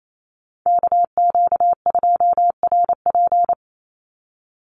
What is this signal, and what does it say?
Morse code, heard as a keyed tone, about 28 wpm, 700 Hz. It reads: KQ2RP
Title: NA SOTA Chaser Callsigns 28wpm